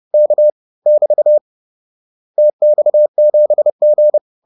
Morse code, heard as a keyed tone, about 30 wpm, 600 Hz. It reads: K <BT> TX7G